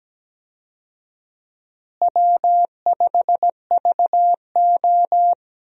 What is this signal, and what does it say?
Morse code, heard as a keyed tone, about 17 wpm, 700 Hz. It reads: W5VO
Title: USA Call Signs - 4 Characters - Repeated 2x - 17wpm